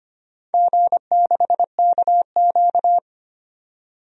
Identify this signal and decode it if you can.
Morse code, heard as a keyed tone, about 25 wpm, 700 Hz. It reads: G6KQ
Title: DXCC-100 Call Signs - 4 Characters - 25wpm